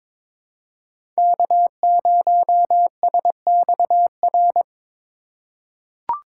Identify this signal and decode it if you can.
Morse code, heard as a keyed tone, about 22 wpm, 700 Hz. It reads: K0SXR
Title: USA Call Signs - 4x Character Spacing - 22wpm